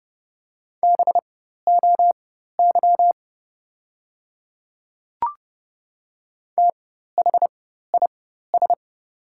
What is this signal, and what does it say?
Morse code, heard as a keyed tone, about 30 wpm, spaced as for 14 wpm, 700 Hz. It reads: BOY THIS
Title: 4x Character Spacing - Top 200 Words - 30wpm